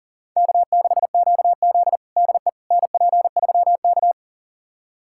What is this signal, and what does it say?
Morse code, heard as a keyed tone, about 40 wpm, 700 Hz. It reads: K6QZ DE NP3K
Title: USA Callsign DE Callsign - 4 Characters - 40wpm